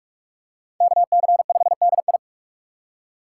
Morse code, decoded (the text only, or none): KC5DI